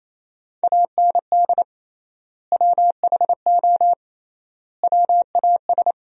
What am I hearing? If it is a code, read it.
Morse code, heard as a keyed tone, about 28 wpm, 700 Hz. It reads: AND WHO WAS
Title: 3x Word-Spacing - Sentences from Top 100 Words - 28wpm